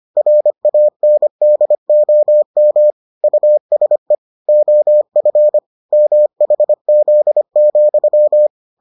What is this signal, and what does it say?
Morse code, heard as a keyed tone, about 25 wpm, 600 Hz. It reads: RANDOM USE OF MHZ,